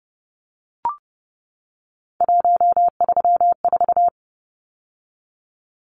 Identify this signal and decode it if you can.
Morse code, heard as a keyed tone, about 30 wpm, 700 Hz. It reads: 134
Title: Numbers - 3-Digit 30wpm